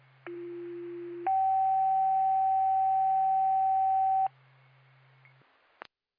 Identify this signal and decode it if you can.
Two-tone page, heard as a one-second tone, then a three-second tone, about 350 then 770 Hz. It is heard from 0.3 to 4.3 s.